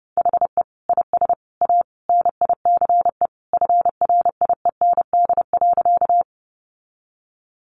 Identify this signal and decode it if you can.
Morse code, heard as a keyed tone, about 30 wpm, 700 Hz. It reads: HE IS A NICE FRIEND.
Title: Sentences from Top 1100 Words 30wpm